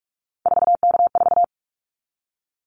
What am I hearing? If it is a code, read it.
Morse code, heard as a keyed tone, about 44 wpm, 700 Hz. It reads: 4K4
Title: Callsign Prefixes for Top 100 DXCC - 45wpm